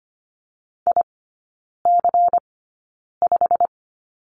Morse code, heard as a keyed tone, about 25 wpm, spaced as for 9 wpm, 700 Hz. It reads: IC5
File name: course / 25:10wpm / 093 Lesson 16e - Letter c - Three Characters 25wpm